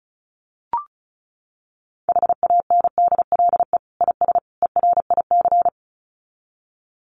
Morse code, heard as a keyed tone, about 34 wpm, 700 Hz. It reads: HANDLE IS ERIC
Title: Common QSO Phrases 35wpm